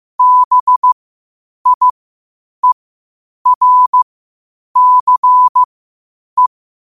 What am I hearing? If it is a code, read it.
Morse code, heard as a keyed tone, about 15 wpm, spaced as for 8 wpm, 1000 Hz. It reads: BIERCE
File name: qotdfiles / 8wpm